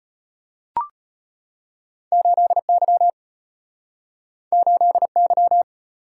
Morse code, heard as a keyed tone, about 34 wpm, 700 Hz. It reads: 8Y 8Y